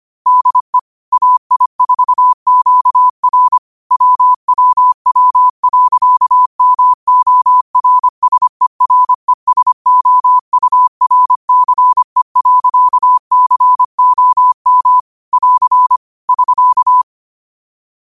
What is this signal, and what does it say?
Morse code, heard as a keyed tone, about 25 wpm, 1000 Hz. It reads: DE AI4QR WWW.MORSERESOURCE.COM <AR> <SK>
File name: qotdfiles / 25wpm